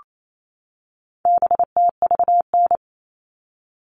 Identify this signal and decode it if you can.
Morse code, heard as a keyed tone, about 28 wpm, 700 Hz. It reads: BTVN
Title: All Cut Numbers - 4-Digit 28wpm